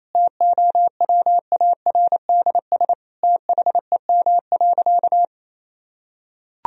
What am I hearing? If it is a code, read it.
Morse code, heard as a keyed tone, about 27 wpm, 700 Hz. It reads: TOWARDS THEM.
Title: Sentences from Top 1000 Words - Review Part 2 - 28wpm